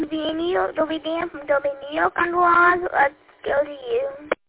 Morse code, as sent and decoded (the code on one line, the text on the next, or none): none